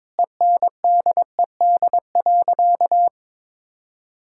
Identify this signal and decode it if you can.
Morse code, heard as a keyed tone, about 21 wpm, 700 Hz. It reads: ENDED.
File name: practice / 22wpm / Sentences with 2-Word Phrases - 22wpm